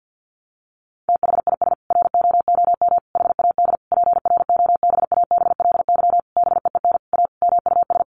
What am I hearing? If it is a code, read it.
Morse code, heard as a keyed tone, about 50 wpm, 700 Hz. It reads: THIS ROOM HAD PROBABLY BEEN A MUS
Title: Sentences from Top 700 Words 50wpm